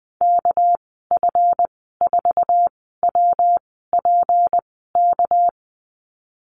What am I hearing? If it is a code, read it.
Morse code, heard as a keyed tone, about 20 wpm, spaced as for 15 wpm, 700 Hz. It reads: KF4WPK
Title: 200 Lesson 32g - Letter k - Call Signs 20wpm